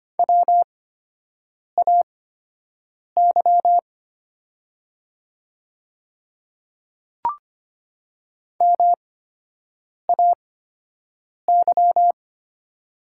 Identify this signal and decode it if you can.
Morse code, heard as a keyed tone, about 25 wpm, spaced as for 7 wpm, 700 Hz. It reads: WAY MAY
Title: 8x Character Spacing - Top 100 Words - 25wpm